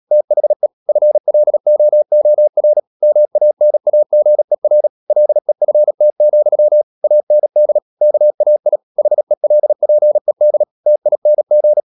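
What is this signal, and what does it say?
Morse code, heard as a keyed tone, about 37 wpm, 600 Hz. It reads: THE FLOOR MANAGER LEFT, AND KAI HELPED TING
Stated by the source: Morse code practice at 35–39 wpm sending English text of Ken Follett's novel Never, from CW med Gnister.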